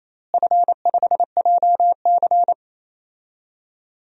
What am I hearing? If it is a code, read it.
Morse code, heard as a keyed tone, about 28 wpm, 700 Hz. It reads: F5JC
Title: DXCC-100 Call Signs - 4 Characters - Repeated 3x - 28wpm